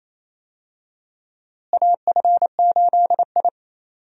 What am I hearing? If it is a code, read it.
Morse code, heard as a keyed tone, about 28 wpm, 700 Hz. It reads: AF8I
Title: USA Call Signs - 4 Characters - Repeated 3x - 28wpm